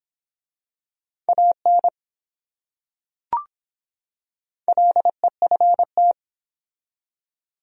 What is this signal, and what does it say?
Morse code, heard as a keyed tone, about 23 wpm, 700 Hz. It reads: AN LEFT